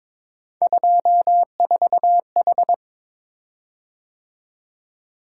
Morse code, heard as a keyed tone, about 22 wpm, 700 Hz. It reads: 24H
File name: practice / 22wpm / Three Letter-Number Combinations - 22wpm